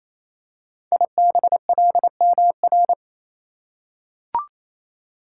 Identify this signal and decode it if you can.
Morse code, heard as a keyed tone, about 29 wpm, 700 Hz. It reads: IBLMR
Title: Code Groups - 5 Letters - 28wpm